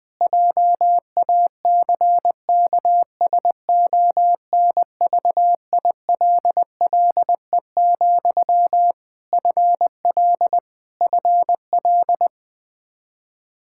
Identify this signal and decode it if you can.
Morse code, heard as a keyed tone, about 20 wpm, 700 Hz. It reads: JACKSONVILLE, FL FL
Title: Practice QSO - Exchange 1 - Beginning - 20wpm